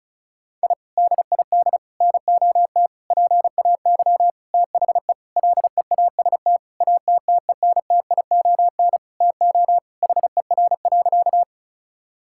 Morse code, heard as a keyed tone, about 36 wpm, 700 Hz. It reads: I DID NOT PAY THE LEAST ATTENTION TO HER.
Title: Sentences from Top 1000 Words - Review Part 6 - 35wpm